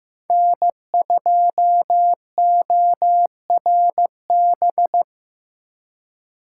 Morse code, heard as a keyed tone, about 18 wpm, 700 Hz. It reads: N2ORB
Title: Speed Racing - USA 1x3 Call Signs 15wpm